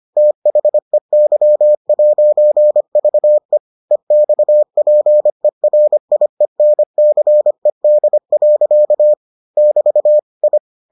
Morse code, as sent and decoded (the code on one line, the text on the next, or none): - .... -.-- .----. ...- . / . -..- .--. .-. .. -. -.-. -.. .-.-.- / -...- / ..
THEY'VE EXPERIENCED. <BT> I